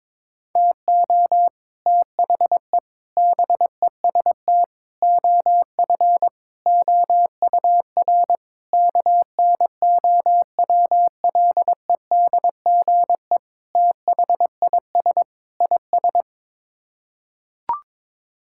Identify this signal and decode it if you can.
Morse code, heard as a keyed tone, about 22 wpm, 700 Hz. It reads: TO THE BEST OF OUR KNOWLEDGE THIS IS
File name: practice / 22wpm / Sets of 8 Words - 22wpm